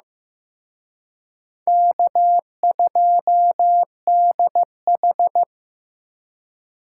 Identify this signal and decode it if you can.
Morse code, heard as a keyed tone, about 18 wpm, 700 Hz. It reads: K2DH
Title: Speed Racing - USA Call Signs 15wpm